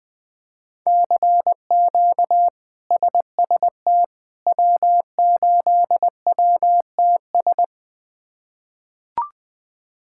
Morse code, heard as a keyed tone, about 20 wpm, 700 Hz. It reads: CQ SST W8WTS
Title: K1USN SST QSO Practice - 20wpm